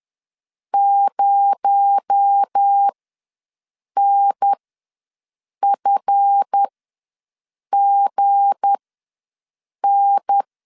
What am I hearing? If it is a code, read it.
Morse code, heard as a keyed tone, about 10 wpm, spaced as for 6 wpm, 800 Hz. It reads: 0NFGN
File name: LEZIONI / 7 GF9Z / 7 GF ws 10wpm 3918